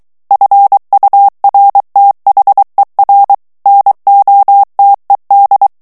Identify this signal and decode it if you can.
Morse code, heard as a keyed tone, about 23 wpm, 800 Hz. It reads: FURTHER NOTED
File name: ANGLAIS texte clair 3 vitesse 22 mots minute